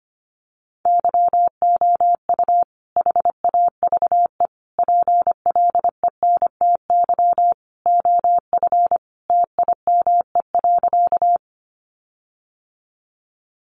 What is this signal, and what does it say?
Morse code, heard as a keyed tone, about 25 wpm, 700 Hz. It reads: YOU HAVE PLENTY OF TIME.
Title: Sentences from Top 1300 Words 25wpm